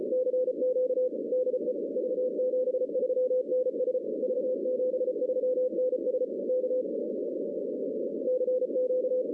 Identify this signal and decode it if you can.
Morse code, heard as a keyed tone, about 34 wpm, 500 Hz, with noise. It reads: CQ DE V73NS V73NS K CQ
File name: 2016 arrl dx cw / 2016-ARRL-DX-CW-K5TR- - 0-25